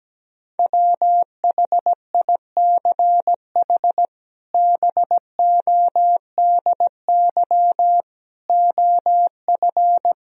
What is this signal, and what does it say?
Morse code, heard as a keyed tone, about 17 wpm, 700 Hz. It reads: WHICH BODY OF
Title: Sentences from Top 1000 Words - Review Part 2 - 17wpm